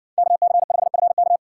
Morse code, heard as a keyed tone, about 60 wpm, 700 Hz. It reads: KC5FK